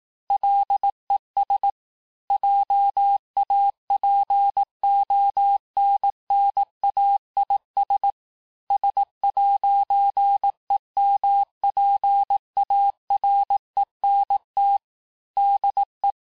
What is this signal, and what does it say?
Morse code, heard as a keyed tone, about 18 wpm, 800 Hz. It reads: LES JAPONNAIS S'EMPARENT DE
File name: texte-clair-vitesse-17mpm-puis-22mpm